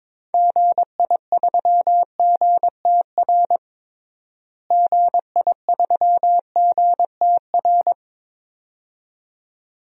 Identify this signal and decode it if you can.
Morse code, heard as a keyed tone, about 22 wpm, 700 Hz. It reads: GI3GTR GI3GTR
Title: DXCC-100 Call Signs - 6 Characters - Repeated 2x - 22wpm